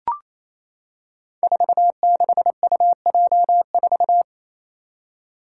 Morse code, heard as a keyed tone, about 28 wpm, 700 Hz. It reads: 46UJ4
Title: Five Challenging Letter-Number Combinations 28wpm